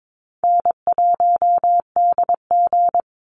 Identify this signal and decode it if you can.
Morse code, heard as a keyed tone, about 22 wpm, 700 Hz. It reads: N1DG
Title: Easy Call Signs - 22wpm